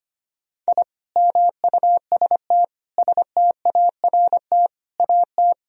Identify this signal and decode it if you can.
Morse code, heard as a keyed tone, about 25 wpm, 700 Hz. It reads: I MUST START AT
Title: Sentences from Top 1000 Words - Review Part 1 - 25wpm